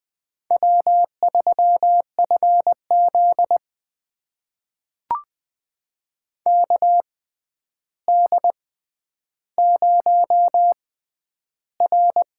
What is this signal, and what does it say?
Morse code, heard as a keyed tone, about 20 wpm, spaced as for 10 wpm, 700 Hz. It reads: W3FZ KD0R
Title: USA Call Signs - 4 Characters - 6x Character Spacing - 20wpm